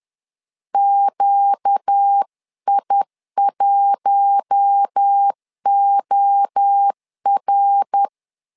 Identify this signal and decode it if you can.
Morse code, heard as a keyed tone, about 10 wpm, 800 Hz. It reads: QI1OR
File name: LEZIONI / 4 Q1 10wpm 2448